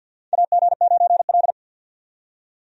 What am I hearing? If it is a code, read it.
Morse code, heard as a keyed tone, about 48 wpm, 700 Hz. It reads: AG9L
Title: USA Call Signs - 4 Characters - 50wpm